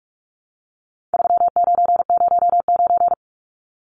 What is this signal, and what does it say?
Morse code, heard as a keyed tone, about 44 wpm, 700 Hz. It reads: 3909
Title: Numbers - 4-Digit 45wpm